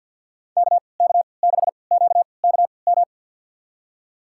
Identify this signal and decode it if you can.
Morse code, heard as a keyed tone, about 53 wpm, spaced as for 29 wpm, 700 Hz. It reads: KK6QKM
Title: USA Call Signs - 6 Characters - 3x Character Spacing - 50wpm